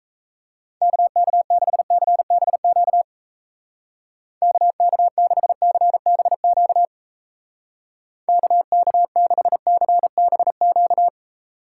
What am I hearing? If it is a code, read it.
Morse code, heard as a keyed tone, about 35 wpm, 700 Hz. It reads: KK6CBQ KK6CBQ KK6CBQ
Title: Speed Racing - USA 2x3 Call Signs 28wpm